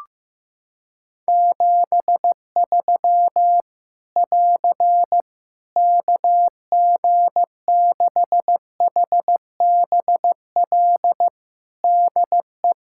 Morse code, heard as a keyed tone, about 15 wpm, 700 Hz. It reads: 73 <AR> KG6HBL DE
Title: Practice QSO - Exchange 4 - Ending - 15wpm